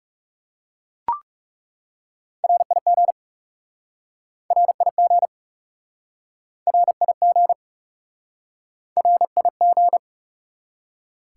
Morse code, heard as a keyed tone, about 41 wpm, 700 Hz. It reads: RIG RIG RIG RIG